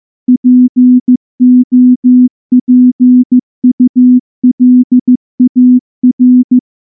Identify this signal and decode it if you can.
Morse code, heard as a keyed tone, about 14 wpm, 250 Hz. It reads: POPULAR
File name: morse-training-15WPM